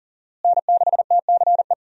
Morse code, heard as a keyed tone, about 40 wpm, 700 Hz. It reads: N6TCE
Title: USA Call Signs - 5 Characters - Repeated 3x - 40wpm